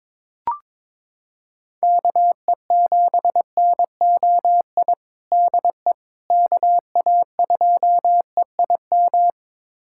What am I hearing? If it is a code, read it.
Morse code, heard as a keyed tone, about 21 wpm, 700 Hz. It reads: KE7NOI DE KA2EIM
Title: USA Callsign DE Callsign - 6 Characters - 22wpm